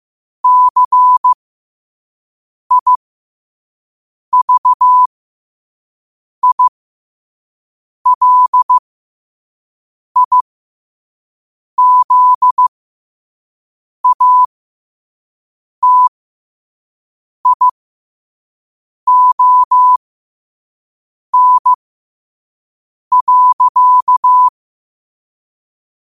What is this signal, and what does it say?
Morse code, heard as a keyed tone, about 15 wpm, spaced as for 5 wpm, 1000 Hz. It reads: CIVILIZATION.
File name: qotdfiles / 5wpm